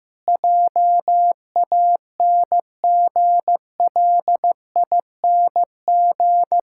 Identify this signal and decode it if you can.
Morse code, heard as a keyed tone, about 16 wpm, 700 Hz. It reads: JANGLING
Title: Pangram Challenge v2 - 15wpm